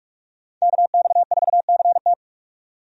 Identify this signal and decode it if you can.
Morse code, heard as a keyed tone, about 46 wpm, 700 Hz. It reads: KX4CT
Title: USA Call Signs - 5 Characters - 45wpm